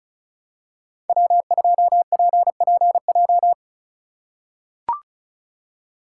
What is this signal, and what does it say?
Morse code, heard as a keyed tone, about 36 wpm, 700 Hz. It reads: W2PPJ
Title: USA Call Signs - 5 Characters - 2x Character Spacing - 35wpm